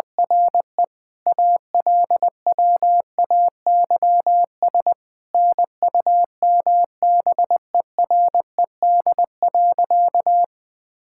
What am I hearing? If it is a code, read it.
Morse code, heard as a keyed tone, about 21 wpm, 700 Hz. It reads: RE ALWAYS NUMBERED.
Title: All-Age Appropriate Jokes - 20wpm